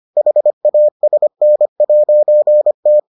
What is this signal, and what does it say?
Morse code, heard as a keyed tone, about 25 wpm, 600 Hz. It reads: HASN'T